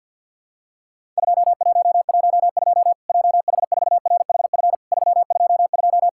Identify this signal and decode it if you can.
Morse code, heard as a keyed tone, about 50 wpm, 700 Hz. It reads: 2112 JH4RHF 322
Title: Worked All Europe DX Contest - QTC Exchange with Repeats - 50wpm